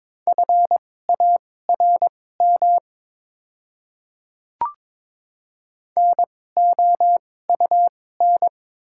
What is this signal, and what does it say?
Morse code, heard as a keyed tone, about 22 wpm, spaced as for 16 wpm, 700 Hz. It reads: FARM NOUN
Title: 2x Character Spacing - Top 500 Words - 22wpm